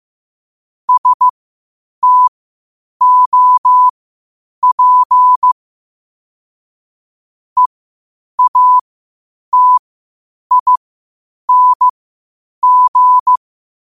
Morse code, heard as a keyed tone, about 15 wpm, spaced as for 8 wpm, 1000 Hz. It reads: STOP EATING
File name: qotdfiles / 8wpm